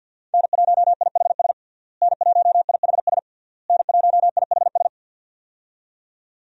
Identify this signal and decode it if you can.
Morse code, heard as a keyed tone, about 51 wpm, 700 Hz. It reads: N1IHS N1IHS N1IHS
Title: USA Call Signs - 5 Characters - Repeated 3x - 50wpm